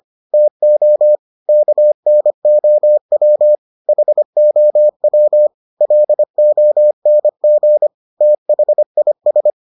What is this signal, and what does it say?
Morse code, heard as a keyed tone, about 25 wpm, 600 Hz. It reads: TO KNOW HOW LONG THIS